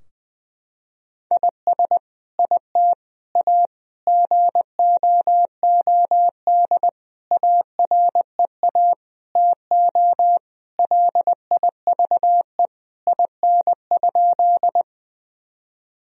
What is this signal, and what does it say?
Morse code, heard as a keyed tone, about 20 wpm, 700 Hz. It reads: IS IT A GOOD AREA TO LIVE IN?
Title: Sentences from Top 1000 Words - Review Part 2 - 20wpm